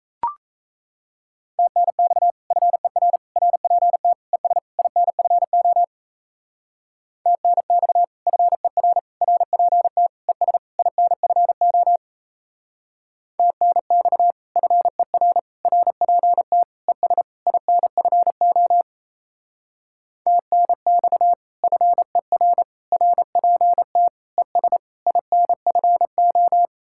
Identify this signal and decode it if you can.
Morse code, heard as a keyed tone, about 34 wpm, 700 Hz. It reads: TNX FER RPT ES INFO TNX FER RPT ES INFO TNX FER RPT ES INFO TNX FER RPT ES INFO
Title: Speed-Racing Common QSO Phrases - 28wpm